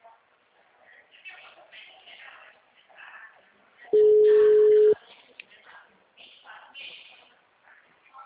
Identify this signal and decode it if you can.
Ringback tone (Europe, 425 Hz).
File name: out- -2041- - - .17809